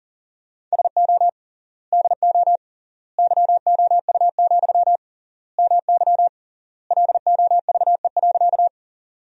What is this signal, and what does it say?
Morse code, heard as a keyed tone, about 40 wpm, 700 Hz. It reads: SO DO YOU, MY LOVE.
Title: 3x Word-Spacing - Sentences from Top 400 Words - 40wpm